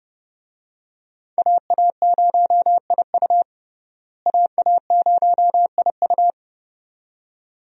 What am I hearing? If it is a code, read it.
Morse code, heard as a keyed tone, about 30 wpm, 700 Hz. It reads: AA0IU AA0IU
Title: USA Call Signs - 5 Characters - Repeated 2x - 30wpm